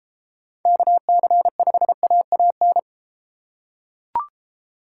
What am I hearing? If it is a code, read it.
Morse code, heard as a keyed tone, about 32 wpm, 700 Hz. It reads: KC5AAN